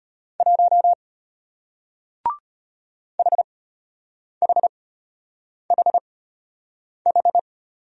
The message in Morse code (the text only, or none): 1 HHHH